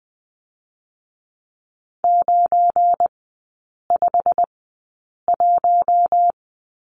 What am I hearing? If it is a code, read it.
Morse code, heard as a keyed tone, about 20 wpm, spaced as for 8 wpm, 700 Hz. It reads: 951
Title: 191 Lesson 31e - Letter v - Three Characters 20wpm